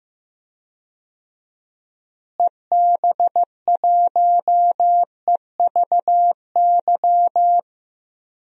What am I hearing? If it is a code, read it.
Morse code, heard as a keyed tone, about 15 wpm, 700 Hz. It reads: EB1EVY
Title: DXCC-100 Call Signs - 6 Characters - Repeated 3x - 15wpm